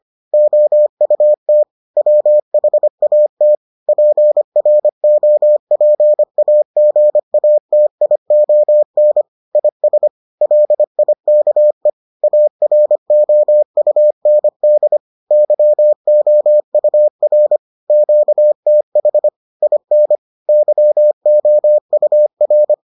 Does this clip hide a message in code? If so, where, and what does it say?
Yes: Morse code from 0.3 to 22.7 s, heard as a keyed tone, about 25 wpm, 600 Hz. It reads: OUT WHAT PROPAGATION IS LIKE AROUND YOUR QTH IN YOUR